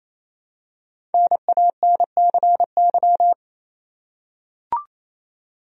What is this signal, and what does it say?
Morse code, heard as a keyed tone, about 28 wpm, 700 Hz. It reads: NANCY
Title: USA Common Names - Max 5 Letters - 28wpm